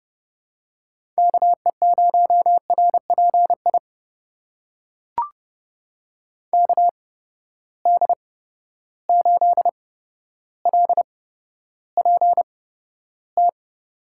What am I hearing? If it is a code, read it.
Morse code, heard as a keyed tone, about 30 wpm, spaced as for 13 wpm, 700 Hz. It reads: KE0RPI KD8LPT
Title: USA Call Signs - 6 Characters - 8x Character Spacing - 30wpm